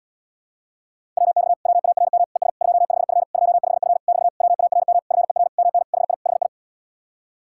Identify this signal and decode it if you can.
Morse code, heard as a keyed tone, about 80 wpm, 700 Hz. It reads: WX WARM ES 25F 25F <BT> WARM FER MN HI HI